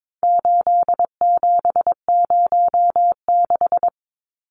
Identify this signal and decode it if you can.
Morse code, heard as a keyed tone, about 22 wpm, 700 Hz. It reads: 8706
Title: Numbers - 4-Digit 22wpm